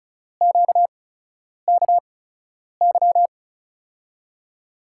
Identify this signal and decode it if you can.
Morse code, heard as a keyed tone, about 36 wpm, spaced as for 10 wpm, 700 Hz. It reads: QKY
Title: USA Call Signs - 8x Character Spacing - 35wpm